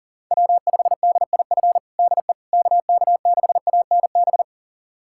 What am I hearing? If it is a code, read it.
Morse code, heard as a keyed tone, about 40 wpm, 700 Hz. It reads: W5DIF DE KK6ANB
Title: USA Callsign DE Callsign 40wpm